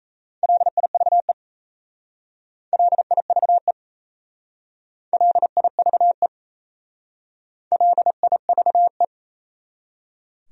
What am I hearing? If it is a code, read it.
Morse code, heard as a keyed tone, about 35 wpm, 700 Hz. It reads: LIVE LIVE LIVE LIVE
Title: Speed-Racing Top 200 Words 28wpm